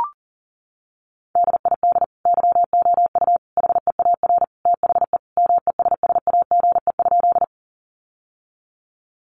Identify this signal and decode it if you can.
Morse code, heard as a keyed tone, about 40 wpm, 700 Hz. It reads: DID YOU HEAR THE MESSAGE?
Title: Sentences from Top 1100 Words 40wpm